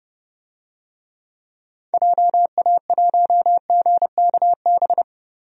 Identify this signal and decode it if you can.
Morse code, heard as a keyed tone, about 32 wpm, 700 Hz. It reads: JA1GKB
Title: DXCC-100 Call Signs - 6 Characters - Repeated 2x - 30wpm